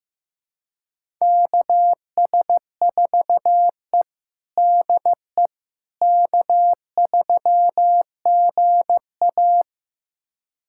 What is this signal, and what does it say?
Morse code, heard as a keyed tone, about 15 wpm, 700 Hz. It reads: KS4E DE K3GA
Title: USA Callsign DE Callsign - 4 Characters - 15wpm